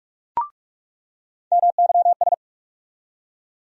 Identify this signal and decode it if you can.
Morse code, heard as a keyed tone, about 46 wpm, 700 Hz. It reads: MYS
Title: USA Call Sign Postfixes - 3 Letters 45wpm